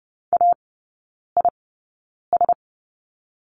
Morse code, heard as a keyed tone, about 30 wpm, spaced as for 9 wpm, 700 Hz. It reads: AIS
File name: Lesson 18e - Letter m - Three Characters 30wpm